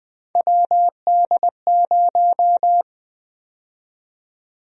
Morse code, heard as a keyed tone, about 20 wpm, 700 Hz. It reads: WD0
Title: USA Call Sign Prefixes - Repeated 2x - 20wpm